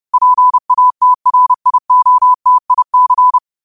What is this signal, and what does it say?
Morse code, heard as a keyed tone, about 30 wpm, 1000 Hz. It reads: PATRIOTIC